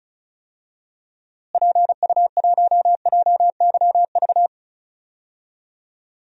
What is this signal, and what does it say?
Morse code, heard as a keyed tone, about 34 wpm, 700 Hz. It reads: PU1JYV
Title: DXCC-100 Call Signs - 6 Characters - Repeated 2x - 35wpm